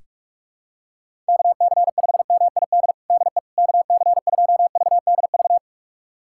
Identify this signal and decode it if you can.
Morse code, heard as a keyed tone, about 42 wpm, 690 Hz. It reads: KC5MID DE KC2VDV
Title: USA Callsign DE Callsign - 6 Characters - 45wpm